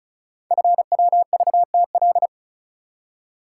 Morse code, heard as a keyed tone, about 36 wpm, 700 Hz. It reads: FWVTL
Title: Code Groups - 5 Letter:Numbers - 35wpm